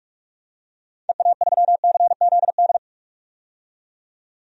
Morse code, heard as a keyed tone, about 41 wpm, 690 Hz. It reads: EA3CZD
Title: DXCC-100 Call Signs - 6 Characters - 45wpm